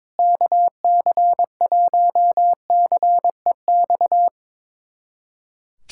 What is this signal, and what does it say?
Morse code, heard as a keyed tone, about 22 wpm, 700 Hz. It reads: KC1CEX
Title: USA Callsign DE Callsign - 6 Characters - 22wpm